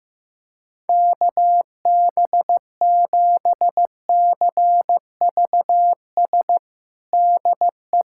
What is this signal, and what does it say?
Morse code, heard as a keyed tone, about 16 wpm, 700 Hz. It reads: KB7CVS DE
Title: USA Callsign DE Callsign - 6 Characters - 15wpm